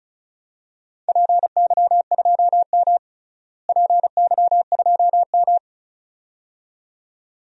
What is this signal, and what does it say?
Morse code, heard as a keyed tone, about 36 wpm, 700 Hz. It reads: PY2M PY2M
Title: DXCC-100 Call Signs - 4 Characters - Repeated 2x - 35wpm